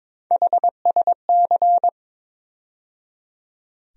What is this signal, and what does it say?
Morse code, heard as a keyed tone, about 22 wpm, 700 Hz. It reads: HSC
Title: USA Call Sign Postfixes - 3 Letters 22wpm